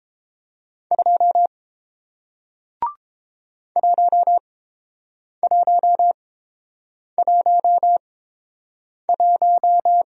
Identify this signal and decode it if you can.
Morse code, heard as a keyed tone, about 30 wpm, spaced as for 7 wpm, 700 Hz. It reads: 2 1111